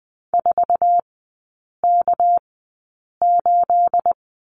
Morse code, heard as a keyed tone, about 20 wpm, spaced as for 8 wpm, 690 Hz. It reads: 4K8
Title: 212 Lesson 34e - Number 8 - Three Characters 20wpm